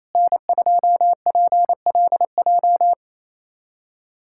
Morse code, heard as a keyed tone, about 28 wpm, 700 Hz. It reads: N2PLJ
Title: USA Callsign DE Callsign 28wpm